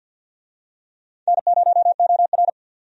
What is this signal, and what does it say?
Morse code, heard as a keyed tone, about 51 wpm, 700 Hz. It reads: N0OR
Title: USA Call Signs - 4 Characters - 50wpm